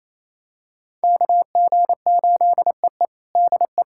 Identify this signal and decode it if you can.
Morse code, heard as a keyed tone, about 27 wpm, 700 Hz. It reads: KG8EE DE
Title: USA Callsign DE Callsign 28wpm